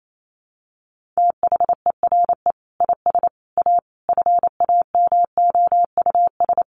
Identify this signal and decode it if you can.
Morse code, heard as a keyed tone, about 28 wpm, 700 Hz. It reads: THERE IS A FAMOUS
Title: Sentences from Top 1500 Words 28wpm